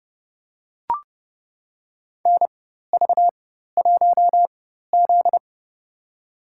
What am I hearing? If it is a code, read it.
Morse code, heard as a keyed tone, about 30 wpm, spaced as for 14 wpm, 700 Hz. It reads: NV1Z
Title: USA Call Signs - 4 Characters - 4x Character Spacing - 30wpm